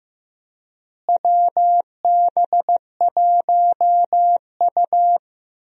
Morse code, heard as a keyed tone, about 15 wpm, 700 Hz. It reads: WB1U
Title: USA Call Signs - 4 Characters - Repeated 3x - 15wpm